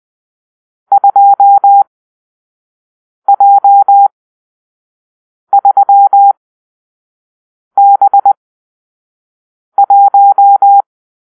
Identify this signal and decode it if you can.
Morse code, heard as a keyed tone, about 20 wpm, spaced as for 5 wpm, 800 Hz. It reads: 2J3B1
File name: arrl / Grp-6 Drill-3